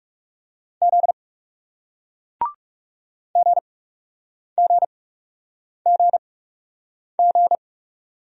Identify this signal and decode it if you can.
Morse code, heard as a keyed tone, about 39 wpm, spaced as for 8 wpm, 700 Hz. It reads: Z GGGG